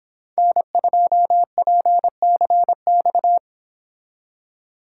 Morse code, heard as a keyed tone, about 25 wpm, 700 Hz. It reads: N2PCX
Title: Speed Racing - USA 1x3 Call Signs 22wpm